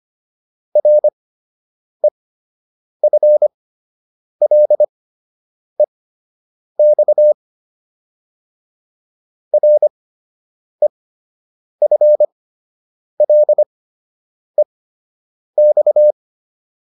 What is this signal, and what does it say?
Morse code, heard as a keyed tone, about 25 wpm, spaced as for 8 wpm, 600 Hz. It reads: REFLEX REFLEX